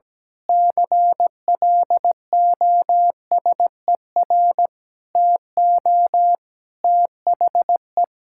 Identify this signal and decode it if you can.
Morse code, heard as a keyed tone, about 17 wpm, 700 Hz. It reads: CLOSER TO THE
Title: Pangram Challenge v2 - 17wpm